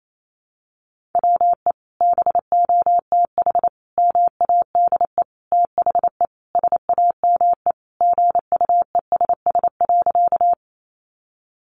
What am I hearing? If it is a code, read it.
Morse code, heard as a keyed tone, about 28 wpm, 700 Hz. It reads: WE BOTH MADE THE SAME GUESS.
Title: Sentences from Top 1500 Words 28wpm